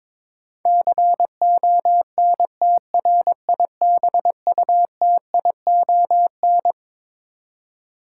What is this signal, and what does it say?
Morse code, heard as a keyed tone, about 22 wpm, 700 Hz. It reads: CONTRIBUTION
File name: Twelve Letter Words - 22wpm